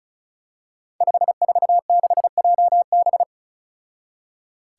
Morse code, heard as a keyed tone, about 35 wpm, 700 Hz. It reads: 546JB
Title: Five Challenging Letter-Number Combinations 35wpm